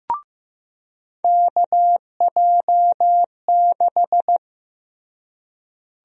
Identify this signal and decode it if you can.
Morse code, heard as a keyed tone, about 15 wpm, 700 Hz. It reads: KJ6
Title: USA Call Sign Prefixes - Repeated 2x - 15wpm